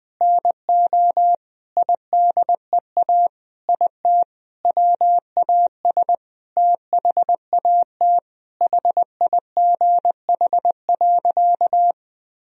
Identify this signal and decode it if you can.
Morse code, heard as a keyed tone, about 19 wpm, 700 Hz. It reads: NO IDEA IT WAS THAT HIGH.
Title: Sentences from Top 1000 Words - Review Part 1 - 20wpm